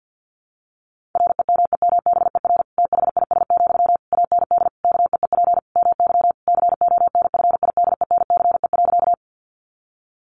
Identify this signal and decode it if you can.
Morse code, heard as a keyed tone, about 50 wpm, 700 Hz. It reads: REMEMBER THIS, AND KEEP MY CONFIDENCE.
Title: Sentences from Top 1500 Words 50wpm